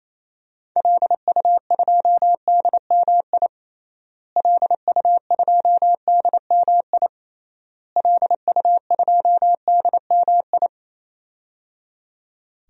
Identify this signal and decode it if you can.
Morse code, heard as a keyed tone, about 27 wpm, 700 Hz. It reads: LU2DMI LU2DMI LU2DMI
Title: DXCC-100 Call Signs - 6 Characters - Repeated 3x - 28wpm